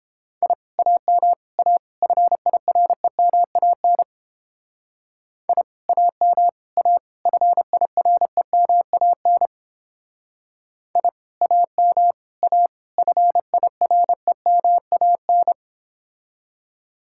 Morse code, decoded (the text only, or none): I AM A FIREMAN I AM A FIREMAN I AM A FIREMAN